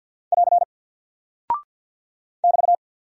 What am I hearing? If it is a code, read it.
Morse code, heard as a keyed tone, about 48 wpm, 700 Hz. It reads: <AR> <BT>
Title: Single Character (Letter:Number:Punctuation:Prosign) 50wpm